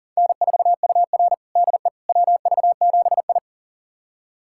Morse code, heard as a keyed tone, about 40 wpm, 700 Hz. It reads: N4UR DE WV7I
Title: USA Callsign DE Callsign - 4 Characters - 40wpm